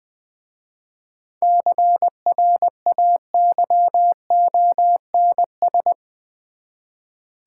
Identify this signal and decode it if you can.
Morse code, heard as a keyed tone, about 21 wpm, 700 Hz. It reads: CRAYONS
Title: Easy Words with Hints - 20wpm